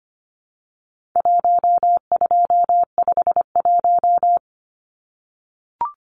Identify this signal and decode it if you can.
Morse code, heard as a keyed tone, about 25 wpm, 700 Hz. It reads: 1251
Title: Numbers - Contest Style - 4-Digit 25wpm